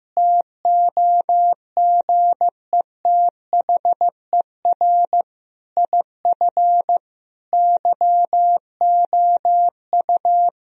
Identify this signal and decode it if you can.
Morse code, heard as a keyed tone, about 15 wpm, 700 Hz. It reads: TOGETHER IF YOU
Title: Sentences from Top 1000 Words - Review Part 2 - 15wpm